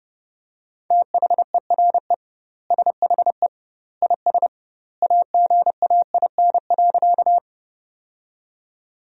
Morse code, heard as a keyed tone, about 30 wpm, 700 Hz. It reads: THERE SHE IS AGAIN.
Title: 2x Word-Spacing - Sentences from Top 200 Words - 30wpm